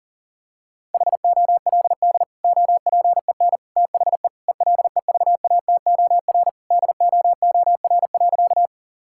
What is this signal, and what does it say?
Morse code, heard as a keyed tone, about 40 wpm, 700 Hz. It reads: HOLD OPEN THE ELEVATOR DOOR.
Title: Sentences from Top 1600 Words 40wpm